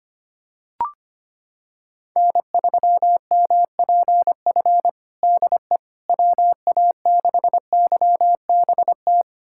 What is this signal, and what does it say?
Morse code, heard as a keyed tone, about 25 wpm, 700 Hz. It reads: N3MPF DE WA6YBT
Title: USA Callsign DE Callsign 25wpm